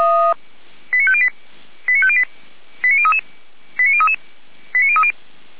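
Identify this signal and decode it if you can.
Five-tone selective call (ZVEI), heard as quick runs of five tones, tones of about 70 ms. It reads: 89508 89508 89033 89033 89030
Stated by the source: ZVEI five-tone call received by SDR, from Feuerwehr Orlamünde fire brigade server.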